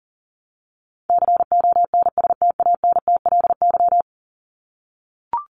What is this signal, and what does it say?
Morse code, heard as a keyed tone, about 40 wpm, 700 Hz. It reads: CONSTANTLY
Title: LY Suffix - Top 100 Words with Base Words 40wpm